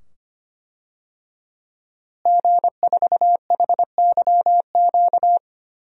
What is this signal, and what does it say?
Morse code, heard as a keyed tone, about 25 wpm, 700 Hz. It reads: G4HYQ
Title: DXCC-100 Call Signs - 5 Characters - Repeated 2x - 25wpm